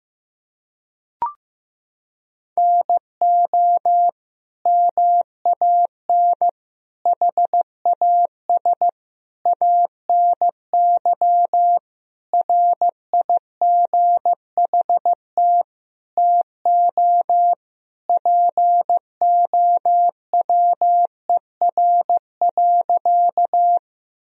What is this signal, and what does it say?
Morse code, heard as a keyed tone, about 15 wpm, 700 Hz. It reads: NO MAN HAS ANY RIGHT TO POWER.
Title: Sentences from Top 1000 Words - Review Part 2 - 15wpm